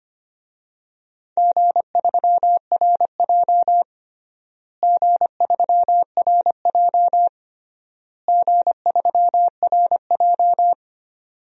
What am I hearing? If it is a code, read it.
Morse code, heard as a keyed tone, about 25 wpm, 700 Hz. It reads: G3RJ G3RJ G3RJ
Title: DXCC-100 Call Signs - 4 Characters - Repeated 3x - 25wpm